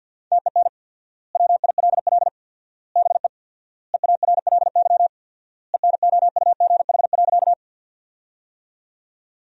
Morse code, heard as a keyed tone, about 52 wpm, 700 Hz. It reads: TEN WILL BE EARLY ENOUGH.
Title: 4x Word-Spacing - Sentences from Top 400 Words - 50wpm